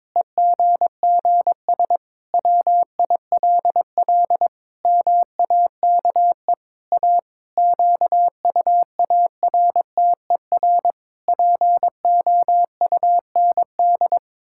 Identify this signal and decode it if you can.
Morse code, heard as a keyed tone, about 22 wpm, 700 Hz. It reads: EGGS WILL MAKE A QUARTER POUND
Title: Pangram Challenge v2 - 22wpm